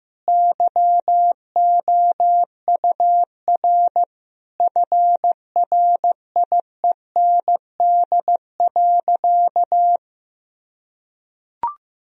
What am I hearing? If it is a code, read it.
Morse code, heard as a keyed tone, about 15 wpm, 700 Hz. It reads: YOUR FRIEND.
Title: Sentences from Top 1000 Words - Review Part 4 - 15wpm